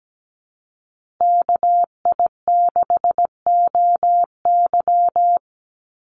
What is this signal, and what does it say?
Morse code, heard as a keyed tone, about 17 wpm, 700 Hz. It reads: KI6OY
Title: Call Signs Repeated 3x - 17wpm